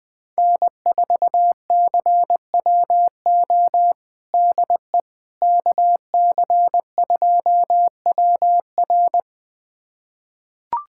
Morse code, heard as a keyed tone, about 21 wpm, 700 Hz. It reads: N4CWO DE KC2WR
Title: USA Callsign DE Callsign - 5 Characters - 20wpm